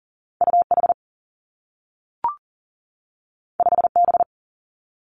Morse code, heard as a keyed tone, about 40 wpm, 700 Hz. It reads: U H 5 B
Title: Two Challenging Letter-Number Combinations 40wpm